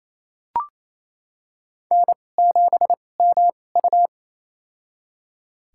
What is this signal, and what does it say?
Morse code, heard as a keyed tone, about 28 wpm, spaced as for 20 wpm, 690 Hz. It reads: N7MU
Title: USA Call Signs - 2x Character Spacing - 28wpm